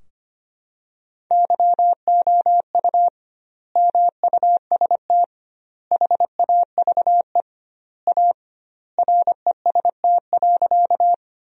Morse code, heard as a keyed tone, about 25 wpm, 700 Hz. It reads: YOU MUST HAVE A REST.
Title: 2x Word-Spacing - Sentences from Top 500 Words - 25wpm